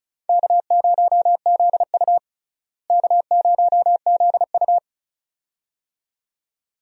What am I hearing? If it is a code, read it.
Morse code, heard as a keyed tone, about 34 wpm, 700 Hz. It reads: K0ZU K0ZU
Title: USA Call Signs - 4 Characters - Repeated 2x - 35wpm